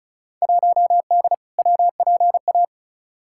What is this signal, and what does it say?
Morse code, heard as a keyed tone, about 34 wpm, 700 Hz. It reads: 1D WPA
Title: Field Day Exchanges - 35wpm